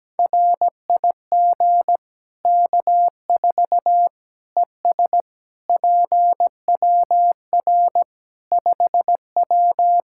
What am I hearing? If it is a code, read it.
Morse code, heard as a keyed tone, about 17 wpm, 700 Hz. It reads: RIG K4 ES PWR 5W
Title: K1USN SST QSO Practice 17wpm